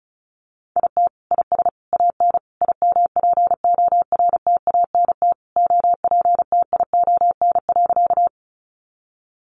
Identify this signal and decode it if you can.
Morse code, heard as a keyed tone, about 34 wpm, 700 Hz. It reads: IT IS AN IMPORTANT OPTION.
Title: Sentences from Top 1300 Words 35wpm